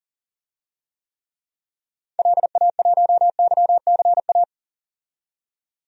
Morse code, heard as a keyed tone, about 40 wpm, 700 Hz. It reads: LA1YCA
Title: DXCC-100 Call Signs - 6 Characters - 40wpm